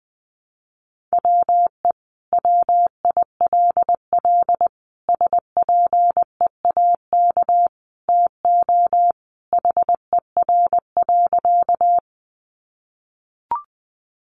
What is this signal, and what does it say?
Morse code, heard as a keyed tone, about 20 wpm, 700 Hz. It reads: WE WILL SPEAK TO HER.
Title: Sentences from Top 1100 Words 20wpm